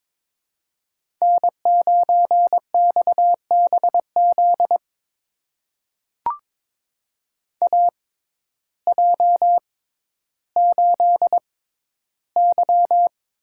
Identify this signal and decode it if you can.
Morse code, heard as a keyed tone, about 22 wpm, spaced as for 12 wpm, 700 Hz. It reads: N9XBZ AJ8Y
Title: USA Call Signs - 6x Character Spacing - 22wpm